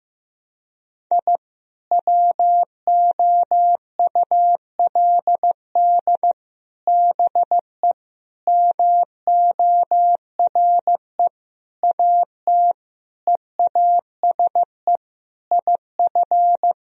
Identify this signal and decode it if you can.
Morse code, heard as a keyed tone, about 15 wpm, 700 Hz. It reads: I WOULD BE MORE AT EASE IF
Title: Sentences from Top 1000 Words - Review Part 1 - 15wpm